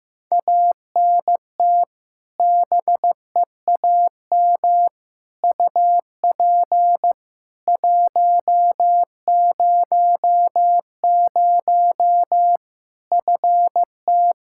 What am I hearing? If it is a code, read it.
Morse code, heard as a keyed tone, about 15 wpm, 700 Hz. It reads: ANT BEAM UP 100 FT
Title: K1USN SST QSO Practice 15wpm